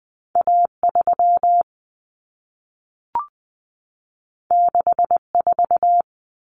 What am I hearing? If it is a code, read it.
Morse code, heard as a keyed tone, about 20 wpm, 700 Hz. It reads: A3 64
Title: Common Cut Numbers - 2-Digit 20wpm